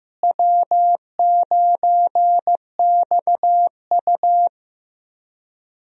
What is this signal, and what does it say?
Morse code, heard as a keyed tone, about 15 wpm, 700 Hz. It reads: W9XU
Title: USA Call Signs - 4 Characters - Repeated 2x - 15wpm